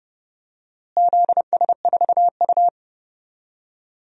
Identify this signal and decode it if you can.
Morse code, heard as a keyed tone, about 30 wpm, 700 Hz. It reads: ZS4U
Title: DXCC-100 Call Signs - 4 Characters - Repeated 2x - 30wpm